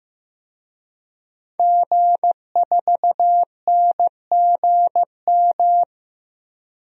Morse code, heard as a keyed tone, about 15 wpm, 700 Hz. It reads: G4NGM
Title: DXCC-100 Call Signs - 5 Characters - 15wpm